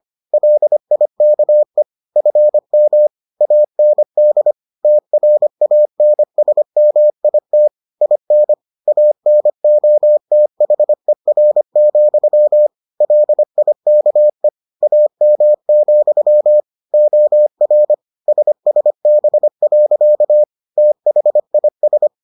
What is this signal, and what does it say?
Morse code, heard as a keyed tone, about 25 wpm, 600 Hz. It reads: LIKE FM AND TRANSMIT IN ANOTHER, LIKE AM, OR SSB. THIS